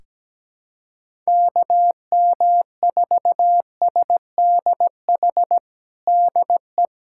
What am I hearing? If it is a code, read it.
Morse code, heard as a keyed tone, about 17 wpm, 700 Hz. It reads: KM4SDH DE
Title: USA Callsign DE Callsign 17wpm